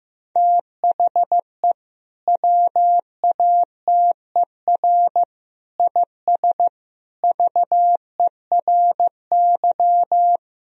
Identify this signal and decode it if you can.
Morse code, heard as a keyed tone, about 15 wpm, 700 Hz. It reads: THE WATER IS VERY
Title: Sentences from Top 1000 Words - Review Part 3 - 15wpm